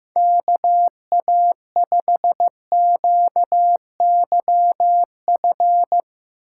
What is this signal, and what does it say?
Morse code, heard as a keyed tone, about 15 wpm, 700 Hz. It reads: KA5QYF
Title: USA Call Signs - 6 Characters - 2x Character Spacing - 15wpm